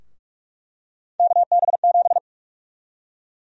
Morse code, heard as a keyed tone, about 46 wpm, 700 Hz. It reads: KB7
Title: USA Call Sign Prefixes - Repeated 2x - 45wpm